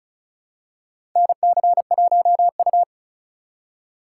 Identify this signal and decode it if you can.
Morse code, heard as a keyed tone, about 36 wpm, 700 Hz. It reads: NC1U
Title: USA Call Signs - 4 Characters - 2x Character Spacing - 35wpm